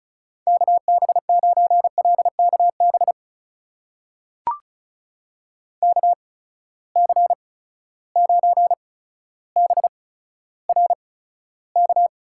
Morse code, heard as a keyed tone, about 36 wpm, spaced as for 15 wpm, 700 Hz. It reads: KB9LKB KC9BRK
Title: USA Call Signs - 6 Characters - 8x Character Spacing - 35wpm